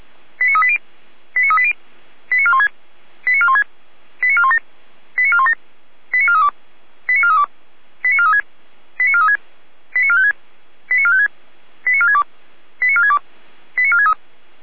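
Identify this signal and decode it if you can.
Five-tone selective call (ZVEI), heard as quick runs of five tones, tones of about 70 ms. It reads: 89380 89380 89416 89416 89417 89417 89432 89432 89436 89436 89456 89456 89462 89462 89463
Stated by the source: ZVEI five-tone selective call received on an SDR.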